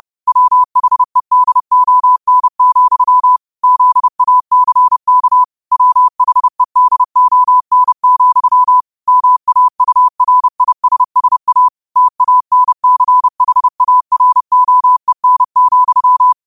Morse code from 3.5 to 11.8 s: ZACK WHEDON, MAURISSA

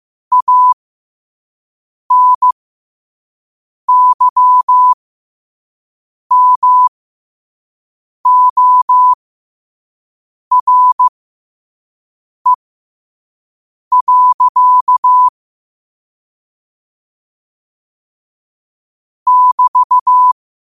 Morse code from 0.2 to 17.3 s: ANYMORE.

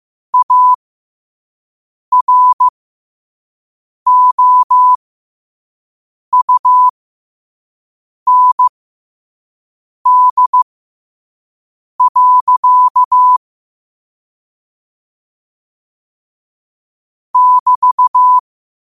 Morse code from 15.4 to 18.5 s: <BT>

